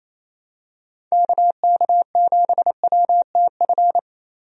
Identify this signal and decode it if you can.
Morse code, heard as a keyed tone, about 28 wpm, 700 Hz. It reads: KK7WTF
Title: USA Call Signs - 6 Characters - Repeated 2x - 28wpm